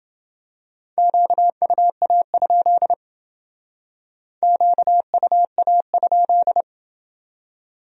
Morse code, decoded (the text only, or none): QUA? QUA?